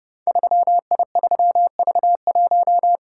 Morse code, heard as a keyed tone, about 30 wpm, 700 Hz. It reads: 3I3V1
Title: Code Groups - 5 Letter:Numbers - 30wpm